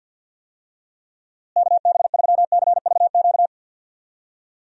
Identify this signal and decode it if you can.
Morse code, heard as a keyed tone, about 51 wpm, 700 Hz. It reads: KB3CVQ
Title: USA Call Signs - 6 Characters - Repeated 3x - 50wpm